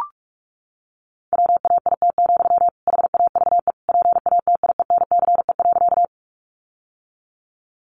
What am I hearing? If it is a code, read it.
Morse code, heard as a keyed tone, about 44 wpm, 700 Hz. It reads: WAIT, HAVE PATIENCE.
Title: Sentences from Top 1300 Words 45wpm